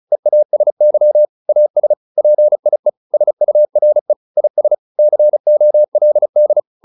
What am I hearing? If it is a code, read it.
Morse code, heard as a keyed tone, about 34 wpm, 600 Hz. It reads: EASY AS PIE SURE IS COLD